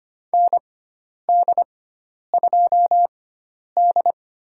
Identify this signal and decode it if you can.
Morse code, heard as a keyed tone, about 25 wpm, spaced as for 10 wpm, 700 Hz. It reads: ND2D
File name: Lesson 16g - Letter c - Call Signs 25wpm